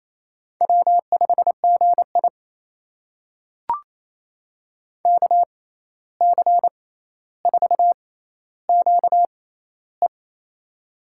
Morse code, heard as a keyed tone, about 28 wpm, spaced as for 13 wpm, 700 Hz. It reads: W5GI KC4QE
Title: USA Call Signs - 6x Character Spacing - 28wpm